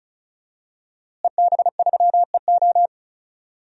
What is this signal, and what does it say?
Morse code, heard as a keyed tone, about 36 wpm, 700 Hz. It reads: EB3EO
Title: DXCC-100 Call Signs - 5 Characters - Repeated 2x - 35wpm